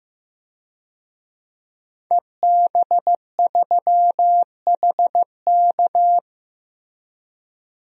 Morse code, heard as a keyed tone, about 15 wpm, 700 Hz. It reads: EB3HK
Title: DXCC-100 Call Signs - 5 Characters - Repeated 2x - 15wpm